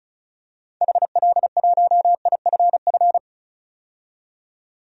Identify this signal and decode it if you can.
Morse code, heard as a keyed tone, about 34 wpm, 700 Hz. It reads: HL1IFF
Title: DXCC-100 Call Signs - 6 Characters - 35wpm